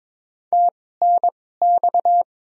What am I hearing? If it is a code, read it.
Morse code, heard as a keyed tone, about 22 wpm, spaced as for 16 wpm, 700 Hz. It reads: TNX
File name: Top 20 QSO Elements - 2x Character Spacing - 22wpm